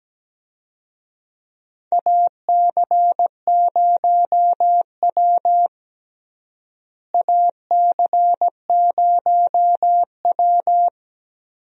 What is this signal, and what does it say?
Morse code, heard as a keyed tone, about 17 wpm, 700 Hz. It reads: AC0W AC0W
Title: USA Call Signs - 4 Characters - Repeated 3x - 17wpm